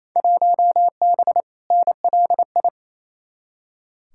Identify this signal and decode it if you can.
Morse code, heard as a keyed tone, about 28 wpm, 700 Hz. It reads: 1B NLI
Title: Field Day Exchanges Encore - 28wpm